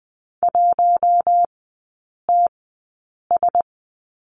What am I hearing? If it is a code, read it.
Morse code, heard as a keyed tone, about 20 wpm, spaced as for 8 wpm, 690 Hz. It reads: 1TS
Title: 060 Lesson 11e - Letter h - Three Characters 20wpm